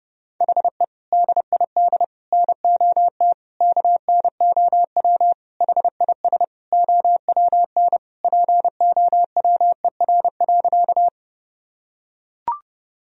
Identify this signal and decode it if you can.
Morse code, heard as a keyed tone, about 32 wpm, 700 Hz. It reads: HE DID NOT KNOW HIS OWN POWER.
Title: Sentences from Top 1000 Words - Review Part 2 - 30wpm